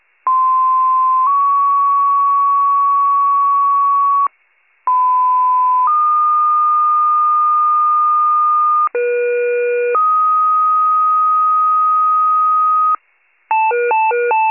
Two-tone page, heard as a one-second tone, then a three-second tone, about 1040 then 1120 Hz, about 1010 then 1220 Hz, about 480 then 1250 Hz. It is heard from 0.3 to 13.0 s.